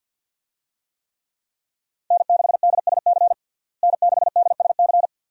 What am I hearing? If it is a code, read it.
Morse code, heard as a keyed tone, about 52 wpm, 700 Hz. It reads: N6DSC N6DSC
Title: USA Call Signs - 5 Characters - Repeated 2x - 50wpm